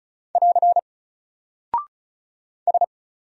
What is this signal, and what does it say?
Morse code, heard as a keyed tone, about 36 wpm, 700 Hz. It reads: <AR> S
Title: Single Character (Letter:Number:Punctuation:Prosign) 35wpm